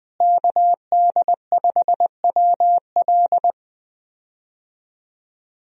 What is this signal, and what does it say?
Morse code, heard as a keyed tone, about 20 wpm, 700 Hz. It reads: KD5WL
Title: USA Call Signs - 5 Characters - Repeated 2x - 20wpm